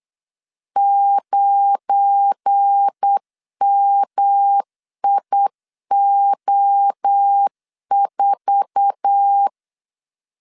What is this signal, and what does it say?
Morse code, heard as a keyed tone, about 8 wpm, 800 Hz. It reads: 9MIO4
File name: LEZIONI / 8 YJ45 / 8 YJ4 8wpm 3246